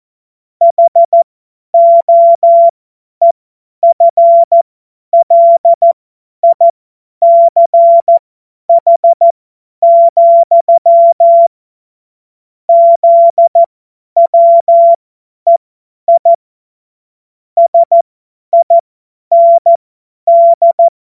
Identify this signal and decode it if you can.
Morse code, heard as a keyed tone, about 14 wpm, spaced as for 10 wpm, 680 Hz. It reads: HOEFLICH, ZWEI SIND
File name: Morsekurs / Witze / Witz 25. Januar 2022, 50 F70